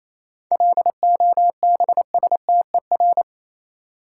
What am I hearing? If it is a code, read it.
Morse code, heard as a keyed tone, about 27 wpm, 700 Hz. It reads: LOBSTER
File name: Easy Words - Speed-Racing with Hints - 22wpm